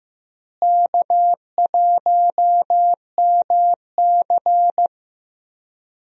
Morse code, heard as a keyed tone, about 15 wpm, 700 Hz. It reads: K1MC
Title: USA Call Signs - 4 Characters - Repeated 3x - 15wpm